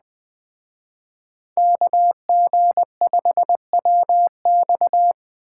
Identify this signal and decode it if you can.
Morse code, heard as a keyed tone, about 21 wpm, 700 Hz. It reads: KG5WX